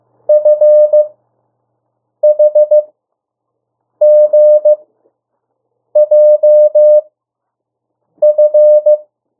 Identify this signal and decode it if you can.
Morse code, heard as a keyed tone, about 15 wpm, spaced as for 6 wpm, 600 Hz. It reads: FHGJF
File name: urok 1